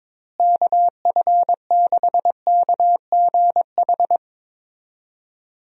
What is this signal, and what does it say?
Morse code, heard as a keyed tone, about 21 wpm, 700 Hz. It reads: KF6KGH
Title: USA Call Signs - 6 Characters - Repeated 3x - 22wpm